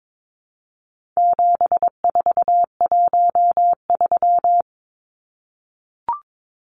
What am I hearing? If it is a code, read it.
Morse code, heard as a keyed tone, about 22 wpm, 700 Hz. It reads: 7413
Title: Numbers - 4-Digit 22wpm